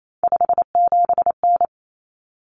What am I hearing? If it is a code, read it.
Morse code, heard as a keyed tone, about 28 wpm, 700 Hz. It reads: 57N
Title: Common Cut Numbers - 3-Digit 28wpm